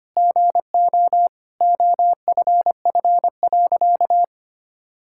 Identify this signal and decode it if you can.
Morse code, heard as a keyed tone, about 25 wpm, 700 Hz. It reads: GO OFF.
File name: practice / 25wpm / Sentences with 4-Word Phrases - 25wpm